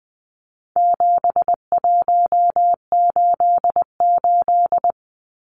Morse code, heard as a keyed tone, about 20 wpm, 700 Hz. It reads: 7188
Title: Numbers - 4-Digit 20wpm